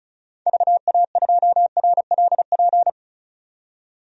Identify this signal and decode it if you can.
Morse code, heard as a keyed tone, about 34 wpm, 700 Hz. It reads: VA2RLP
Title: DXCC-100 Call Signs - 6 Characters - 35wpm